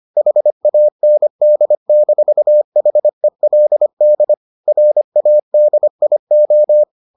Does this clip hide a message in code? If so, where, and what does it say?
Yes: Morse code from 0.2 to 6.8 s, heard as a keyed tone, about 25 wpm, 600 Hz. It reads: HAND-HELD RADIO